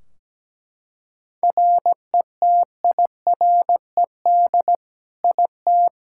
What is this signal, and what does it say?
Morse code, heard as a keyed tone, about 17 wpm, 700 Hz. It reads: RETIRED IT
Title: Practice QSO - Exchange 3 - More Info - 17wpm